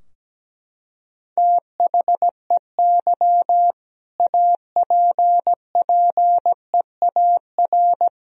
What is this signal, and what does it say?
Morse code, heard as a keyed tone, about 17 wpm, 700 Hz. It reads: THEY APPEAR
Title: Sentences from Top 1000 Words - Review Part 2 - 17wpm